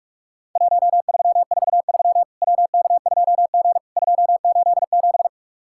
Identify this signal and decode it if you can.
Morse code, heard as a keyed tone, about 44 wpm, 700 Hz. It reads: 1343 WK2G 287
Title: Worked All Europe DX Contest - QTC Exchange - 45wpm